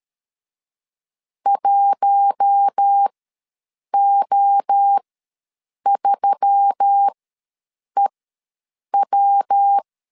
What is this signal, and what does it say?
Morse code, heard as a keyed tone, about 13 wpm, spaced as for 7 wpm, 800 Hz. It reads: 1O3EW
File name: LEZIONI / 5 LBVC / 5 L ws 12wpm 2958